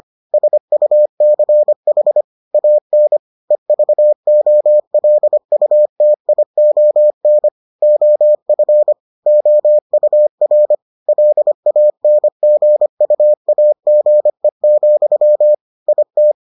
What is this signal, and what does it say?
Morse code, heard as a keyed tone, about 25 wpm, 600 Hz. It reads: SUCH AN EVOLUTION OF OUR LANGUAGE, IT